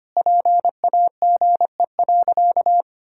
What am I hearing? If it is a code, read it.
Morse code, heard as a keyed tone, about 25 wpm, 700 Hz. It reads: PAGE.
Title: Sentences from Top 1000 Words - Review Part 1 - 25wpm